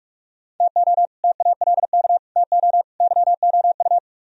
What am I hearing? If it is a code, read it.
Morse code, heard as a keyed tone, about 55 wpm, 700 Hz. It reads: TO TALK TO YOU
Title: Speed-Racing - Sets of 4 Words 45wpm